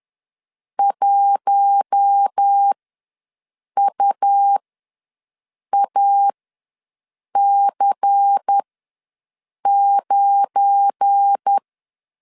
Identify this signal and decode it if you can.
Morse code, heard as a keyed tone, about 10 wpm, spaced as for 6 wpm, 800 Hz. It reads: 1UAC9
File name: LEZIONI / 7 GF9Z / 7 GF9 ws 10wpm 4034